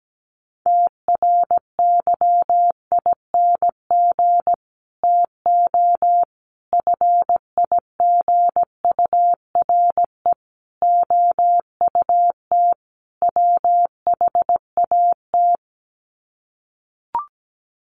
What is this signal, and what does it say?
Morse code, heard as a keyed tone, about 17 wpm, 700 Hz. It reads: TRYING TO FIGURE OUT WHAT
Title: Sets of 5 Words 17wpm